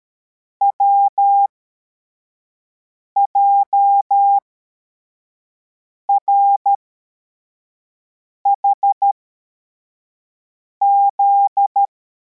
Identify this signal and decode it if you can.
Morse code, heard as a keyed tone, about 13 wpm, spaced as for 4 wpm, 800 Hz. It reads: WJRHZ
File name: formation / cw / Text84